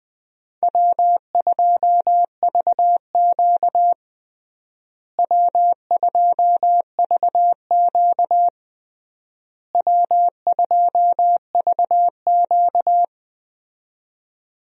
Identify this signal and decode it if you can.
Morse code, heard as a keyed tone, about 20 wpm, 700 Hz. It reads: W2VQ W2VQ W2VQ
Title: USA Call Signs - 4 Characters - Repeated 3x - 20wpm